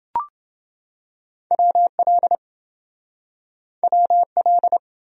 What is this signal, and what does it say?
Morse code, heard as a keyed tone, about 26 wpm, 700 Hz. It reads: WL WL